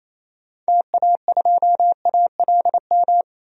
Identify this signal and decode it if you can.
Morse code, heard as a keyed tone, about 28 wpm, 700 Hz. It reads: TA2ALM
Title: DXCC-100 Call Signs - 6 Characters - 28wpm